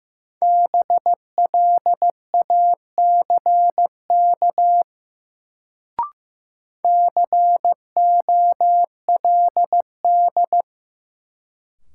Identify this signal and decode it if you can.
Morse code, heard as a keyed tone, about 15 wpm, 700 Hz. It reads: BLACK COLD
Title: Sentences from Top 400 Words - New Word Drill - 15wpm